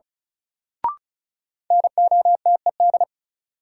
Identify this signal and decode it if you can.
Morse code, heard as a keyed tone, about 36 wpm, 700 Hz. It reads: NOTED
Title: Sets of 5 Words - New Word Drill - 35wpm